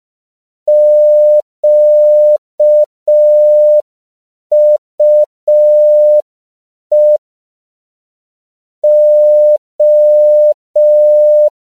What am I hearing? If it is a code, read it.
Morse code, heard as a keyed tone, about 5 wpm, 600 Hz. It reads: QUE O
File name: CW / 5PPM